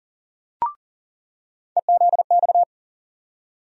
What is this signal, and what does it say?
Morse code, heard as a keyed tone, about 40 wpm, 700 Hz. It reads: EZX
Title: USA Call Sign Postfixes - 3 Letters 40wpm